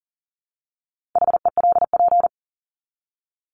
Morse code, heard as a keyed tone, about 40 wpm, 700 Hz. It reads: HELP
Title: Top 200 Words 40wpm